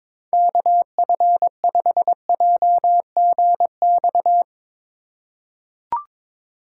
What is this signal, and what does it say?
Morse code, heard as a keyed tone, about 22 wpm, 700 Hz. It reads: KF5JGX
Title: USA Call Signs - 6 Characters - 4x Character Spacing - 22wpm